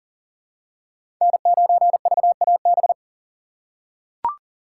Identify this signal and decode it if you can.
Morse code, heard as a keyed tone, about 40 wpm, 700 Hz. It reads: N9VAB
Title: USA Call Signs - 4x Character Spacing - 40wpm